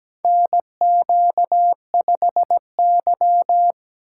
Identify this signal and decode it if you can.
Morse code, heard as a keyed tone, about 17 wpm, 700 Hz. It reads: NQ5Y
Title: USA Call Signs - 4 Characters - 3x Character Spacing - 17wpm